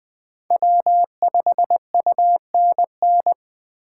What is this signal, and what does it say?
Morse code, heard as a keyed tone, about 20 wpm, 700 Hz. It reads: W5UNN
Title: USA Call Signs - 5 Characters - 8x Character Spacing - 20wpm